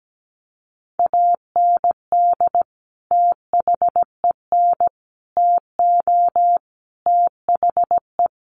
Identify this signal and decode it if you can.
Morse code, heard as a keyed tone, about 17 wpm, 700 Hz. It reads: AND THEN TO THE
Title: Sentences from Top 1000 Words 17wpm